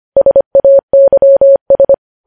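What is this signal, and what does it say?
Morse code, heard as a keyed tone, about 25 wpm, 550 Hz. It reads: SAYS
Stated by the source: Morse code news headlines sent at 25 words per minute.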